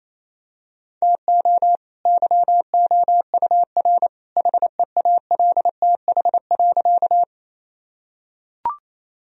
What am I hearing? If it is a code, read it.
Morse code, heard as a keyed tone, about 28 wpm, 700 Hz. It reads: TO YOUR HEALTH.
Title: Sentences from Top 1000 Words - Review Part 3 - 28wpm